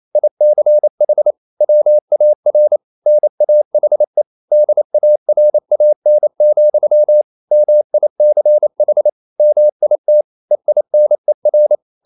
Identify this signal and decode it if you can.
Morse code, heard as a keyed tone, about 28 wpm, 600 Hz. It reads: ICH WAR NAHE DARAN, MICH MIT EINER
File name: Liebespaar4 28wpm